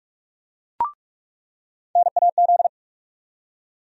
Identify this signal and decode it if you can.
Morse code, heard as a keyed tone, about 46 wpm, 700 Hz. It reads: NAZ
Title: USA Call Sign Postfixes - 3 Letters 45wpm